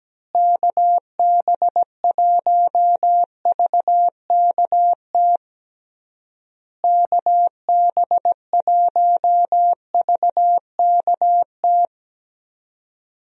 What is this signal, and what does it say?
Morse code, heard as a keyed tone, about 17 wpm, 700 Hz. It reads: KB1VKT KB1VKT
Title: USA Call Signs - 6 Characters - Repeated 3x - 17wpm